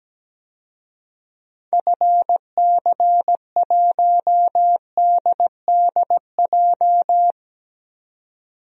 Morse code, heard as a keyed tone, about 17 wpm, 700 Hz. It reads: FC1DDJ
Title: DXCC-100 Call Signs - 6 Characters - 17wpm